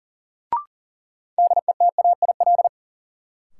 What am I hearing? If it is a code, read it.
Morse code, heard as a keyed tone, about 40 wpm, 700 Hz. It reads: DETAIL
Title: Sentences from Top 1500 Words - New Word Drill - 40wpm